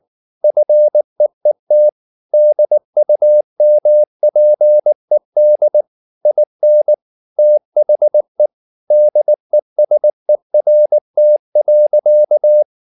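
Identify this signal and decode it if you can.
Morse code, heard as a keyed tone, about 19 wpm, 600 Hz. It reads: FEET DUMPED IN THE DESERT.
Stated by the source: Morse code at 19 wpm from CW med Gnister, reading Ken Follett's novel Never in English.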